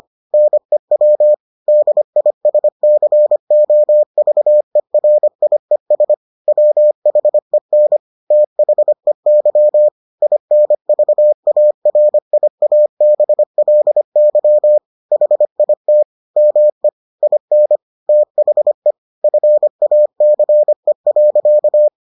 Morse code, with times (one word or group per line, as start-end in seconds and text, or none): 0.3-1.3 NEW
1.7-6.1 DISCOVERIES
6.5-8.0 WHEN
8.3-9.9 THEY
10.2-14.8 INVARIABLY
15.1-16.0 HIT
16.4-16.9 ME
17.2-17.8 IN
18.1-18.9 THE
19.2-21.9 FACE.